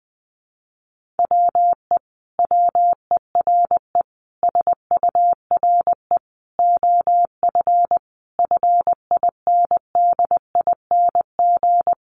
Morse code, heard as a keyed tone, about 20 wpm, 700 Hz. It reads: WE WERE SURE OF FINDING
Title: Sentences from Top 1100 Words 20wpm